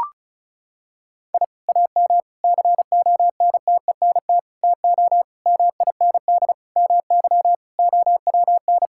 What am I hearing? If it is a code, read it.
Morse code, heard as a keyed tone, about 36 wpm, 700 Hz. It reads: I AM CONTENT TO MIND MY OWN
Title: Sentences from Top 1000 Words - Review Part 6 - 35wpm